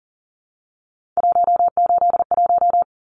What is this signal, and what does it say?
Morse code, heard as a keyed tone, about 40 wpm, 700 Hz. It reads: 181
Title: Numbers - 3-Digit 40wpm